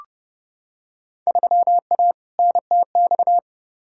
Morse code, heard as a keyed tone, about 30 wpm, 700 Hz. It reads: 3A NTX
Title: Field Day Exchanges - 30wpm